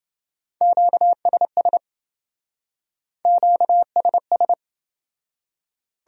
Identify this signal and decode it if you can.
Morse code, heard as a keyed tone, about 28 wpm, 700 Hz. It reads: QSS QSS